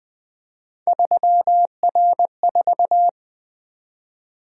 Morse code, heard as a keyed tone, about 24 wpm, 700 Hz. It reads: 3R4
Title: Speed-Racing - Three Letter-Number Combinations - 20wpm